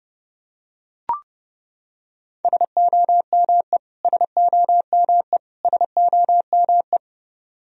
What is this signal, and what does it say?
Morse code, heard as a keyed tone, about 30 wpm, 700 Hz. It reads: SOME SOME SOME
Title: Top 100 Words Repeated 30wpm